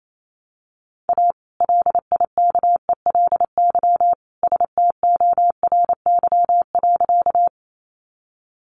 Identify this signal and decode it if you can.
Morse code, heard as a keyed tone, about 28 wpm, 700 Hz. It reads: A LIKELY STORY.
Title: Sentences from Top 600 Words 28wpm